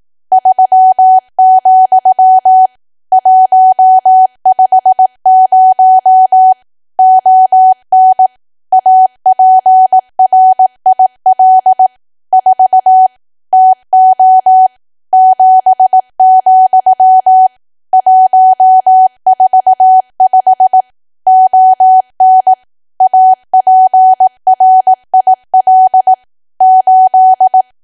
Morse code, heard as a keyed tone, about 18 wpm, 750 Hz. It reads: 3, 150 ON APRIL 4 TO 7, 145 ON APRIL 8
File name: cwa / W1AW-Level2 / 18WPM